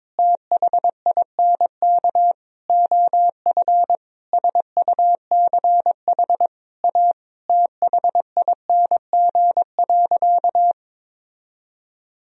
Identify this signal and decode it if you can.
Morse code, heard as a keyed tone, about 22 wpm, 700 Hz. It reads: THINK OF SUCH A THING.
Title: Sentences from Top 1000 Words - Review Part 1 - 22wpm